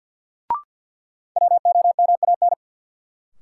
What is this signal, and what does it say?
Morse code, heard as a keyed tone, about 51 wpm, 700 Hz. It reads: WOMAN
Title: Sentences from Top 1000 Words - Review Part 2 - New Word Drill - 50wpm